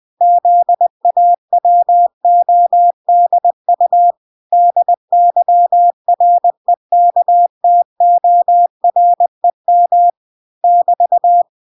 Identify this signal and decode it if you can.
Morse code, heard as a keyed tone, about 20 wpm, 700 Hz. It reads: ZAWODU DYREKTOREM <BT>